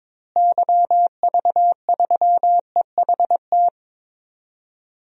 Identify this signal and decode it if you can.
Morse code, heard as a keyed tone, about 22 wpm, 700 Hz. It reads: YV3EHT
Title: DXCC-100 Call Signs - 6 Characters - Repeated 3x - 22wpm